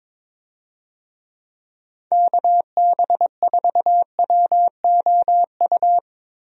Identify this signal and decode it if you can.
Morse code, heard as a keyed tone, about 22 wpm, 700 Hz. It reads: KB4WOU
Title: USA Call Signs - 6 Characters - 22wpm